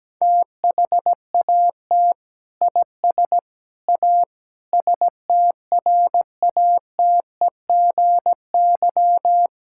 Morse code, heard as a keyed tone, about 17 wpm, 700 Hz. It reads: THAT IS A STRATEGY
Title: Sentences from Top 1000 Words - Review Part 5 - 17wpm